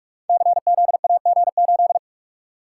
Morse code, heard as a keyed tone, about 42 wpm, 700 Hz. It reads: CZAG8
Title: Code Groups - 5 Letter:Numbers - 45wpm